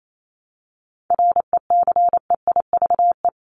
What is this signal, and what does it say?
Morse code, heard as a keyed tone, about 28 wpm, 700 Hz. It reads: RECEIVE
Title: Top 751-1000 Words 28wpm